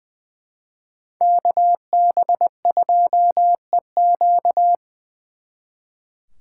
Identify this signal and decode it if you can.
Morse code, heard as a keyed tone, about 24 wpm, 700 Hz. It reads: KB2EQ
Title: Speed Racing - USA 2x2 Call Signs 20wpm